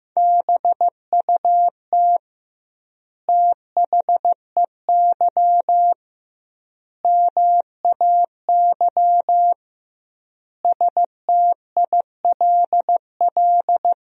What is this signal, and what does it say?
Morse code, heard as a keyed tone, about 15 wpm, 700 Hz. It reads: BUT THEY MAY STILL
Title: 2x Word-Spacing - Sentences from Top 300 Words - 15wpm